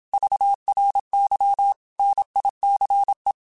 Morse code, heard as a keyed tone, about 27 wpm, 800 Hz. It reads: VRY NICE
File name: QSOs / QSOs-mp3s / kc7sac@25